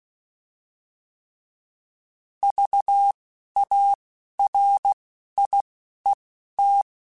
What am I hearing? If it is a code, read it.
Morse code, heard as a keyed tone, about 16 wpm, 800 Hz. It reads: V A R I E T
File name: Alphabetical / Alphabetical mp3s / Vs @10